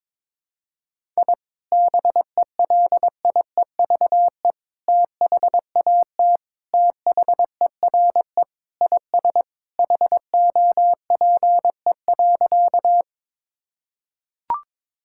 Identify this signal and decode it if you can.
Morse code, heard as a keyed tone, about 22 wpm, 700 Hz. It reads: I BELIEVE THAT THERE IS HOPE.
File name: practice / 22wpm / Sentences with 3-Word Phrases - 22wpm